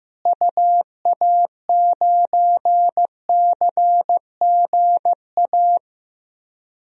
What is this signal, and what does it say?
Morse code, heard as a keyed tone, about 16 wpm, 700 Hz. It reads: UA9CGA
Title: DXCC-100 Call Signs - 6 Characters - Repeated 3x - 15wpm